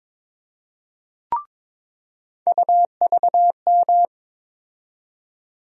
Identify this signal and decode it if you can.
Morse code, heard as a keyed tone, about 22 wpm, 700 Hz. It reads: UVM
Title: USA Call Sign Postfixes - 3 Letters 22wpm